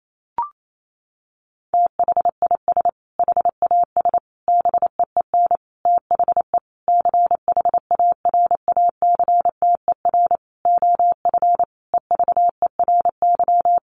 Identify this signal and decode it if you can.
Morse code, heard as a keyed tone, about 28 wpm, 700 Hz. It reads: THIS HAS BEEN THE CHARACTER OF EVERY
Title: Sentences from Top 1400 Words 28wpm